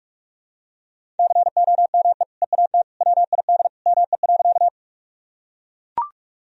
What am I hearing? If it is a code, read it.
Morse code, heard as a keyed tone, about 44 wpm, 700 Hz. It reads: COME EAT WID ME.
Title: Sentences from Top 300 Words and Abbreviations 45wpm